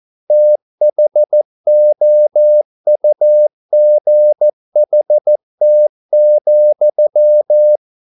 Morse code, heard as a keyed tone, about 14 wpm, 600 Hz. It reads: THOUGHT,